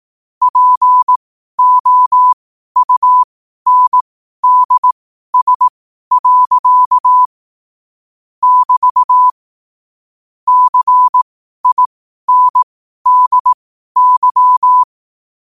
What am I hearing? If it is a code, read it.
Morse code, heard as a keyed tone, about 18 wpm, spaced as for 12 wpm, 1000 Hz. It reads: POUNDS. <BT> CINDY
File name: qotdfiles / 12wpm